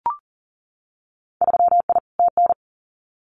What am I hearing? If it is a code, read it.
Morse code, heard as a keyed tone, about 40 wpm, 700 Hz. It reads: 3I TN
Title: Winter Field Day Exchanges 40wpm